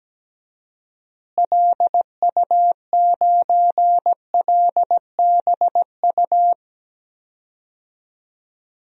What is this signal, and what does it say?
Morse code, heard as a keyed tone, about 17 wpm, 700 Hz. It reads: LU9LBU
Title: DXCC-100 Call Signs - 6 Characters - Repeated 2x - 17wpm